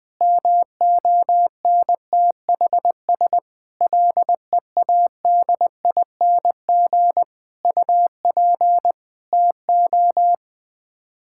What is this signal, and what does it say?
Morse code, heard as a keyed tone, about 20 wpm, 700 Hz. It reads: MONTHS LEADING UP TO
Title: Sets of 6 Words - 20wpm